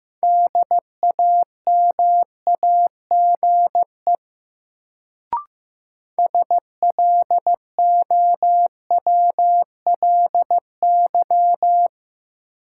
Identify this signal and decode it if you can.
Morse code, heard as a keyed tone, about 15 wpm, 700 Hz. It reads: DAMAGE SLOWLY
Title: Sentences from Top 1100 Words - New Word Drill - 15wpm